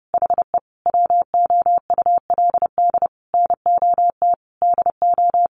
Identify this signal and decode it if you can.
Morse code, heard as a keyed tone, about 30 wpm, 700 Hz. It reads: HE WOULD NOT DO
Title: Sentences from Top 600 Words 30wpm